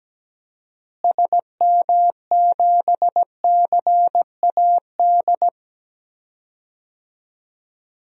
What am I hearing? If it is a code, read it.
Morse code, heard as a keyed tone, about 17 wpm, 700 Hz. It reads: SM7CAD
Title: DXCC-100 Call Signs - 6 Characters - Repeated 2x - 17wpm